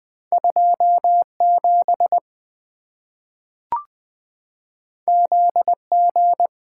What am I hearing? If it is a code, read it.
Morse code, heard as a keyed tone, about 20 wpm, 700 Hz. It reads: 27 ZG
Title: Two Letter-Number Combinations - 20wpm